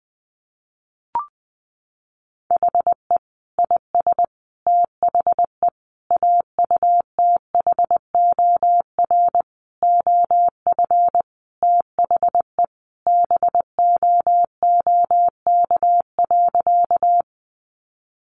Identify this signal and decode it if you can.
Morse code, heard as a keyed tone, about 20 wpm, 700 Hz. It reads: HE IS THE AUTHOR OF THE BOOK.
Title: Sentences from Top 1200 Words 20wpm